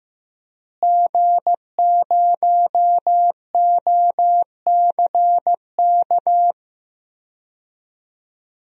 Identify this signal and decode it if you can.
Morse code, heard as a keyed tone, about 15 wpm, 700 Hz. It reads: G0OCK
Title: DXCC-100 Call Signs - 5 Characters - Repeated 2x - 15wpm